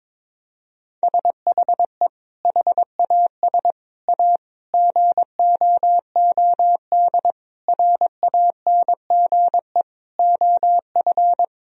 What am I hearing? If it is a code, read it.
Morse code, heard as a keyed tone, about 22 wpm, 700 Hz. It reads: SHE HAS A GOOD RANGE OF
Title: Sentences from Top 1000 Words - Review Part 4 - 22wpm